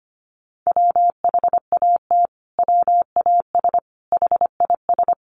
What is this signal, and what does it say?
Morse code, heard as a keyed tone, about 25 wpm, 700 Hz. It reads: WHAT WAS HIS
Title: Sentences from Top 1400 Words 25wpm